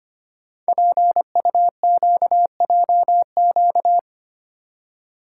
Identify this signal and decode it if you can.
Morse code, heard as a keyed tone, about 25 wpm, 700 Hz. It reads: PUQJQ
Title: Code Groups - 5 Letters - 25wpm